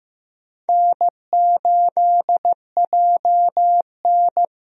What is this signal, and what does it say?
Morse code, heard as a keyed tone, about 15 wpm, 700 Hz. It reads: N8JN
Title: USA Call Signs - 4 Characters - Repeated 2x - 15wpm